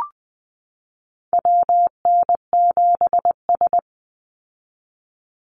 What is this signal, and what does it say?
Morse code, heard as a keyed tone, about 20 wpm, 700 Hz. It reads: WN7S
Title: Call Signs Repeated 2x - 20wpm